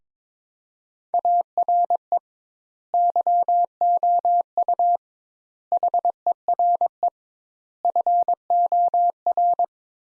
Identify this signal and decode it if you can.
Morse code, heard as a keyed tone, about 22 wpm, 700 Hz. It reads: ARE YOU HERE FOR
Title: Sentences Top 200 Words - 22wpm